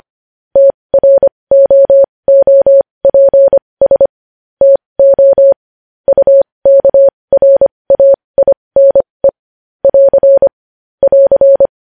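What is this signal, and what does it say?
Morse code, heard as a keyed tone, about 25 wpm, spaced as for 20 wpm, 550 Hz. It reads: TROOPS TO UKRAINE <AR> <AR>
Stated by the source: Morse code news headlines.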